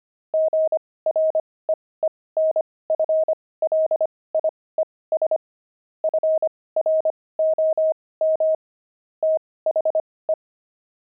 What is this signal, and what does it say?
Morse code, heard as a keyed tone, about 25 wpm, spaced as for 18 wpm, 630 Hz. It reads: GREENFLIES FROM THE
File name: pangram-fs-18-cs-25-beehive